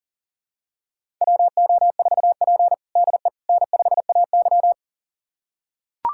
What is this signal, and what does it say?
Morse code, heard as a keyed tone, about 40 wpm, 700 Hz. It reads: WO4P DE N5AY
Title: USA Callsign DE Callsign - 4 Characters - 40wpm